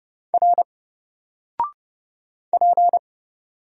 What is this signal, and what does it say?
Morse code, heard as a keyed tone, about 30 wpm, 700 Hz. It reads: R P
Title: Single Character (Letter:Number:Punctuation:Prosign) 30wpm